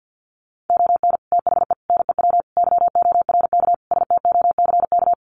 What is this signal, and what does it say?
Morse code, heard as a keyed tone, about 50 wpm, 700 Hz. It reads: ON THE NEW YORK STOCK
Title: Sets of 5 Words Encore 50wpm